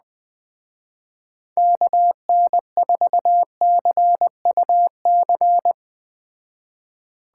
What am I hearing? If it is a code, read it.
Morse code, heard as a keyed tone, about 20 wpm, 700 Hz. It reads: KN4CUC
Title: Speed Racing - USA 2x3 Call Signs 17wpm